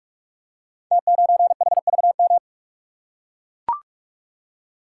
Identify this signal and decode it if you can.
Morse code, heard as a keyed tone, about 44 wpm, 700 Hz. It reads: T9HVM
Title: Code Groups - 5 Letter:Numbers - 45wpm